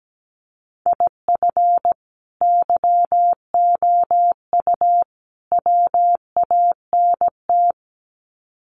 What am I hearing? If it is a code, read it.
Morse code, heard as a keyed tone, about 17 wpm, 700 Hz. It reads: IF YOU WANT
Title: Sets of 3 Words 17wpm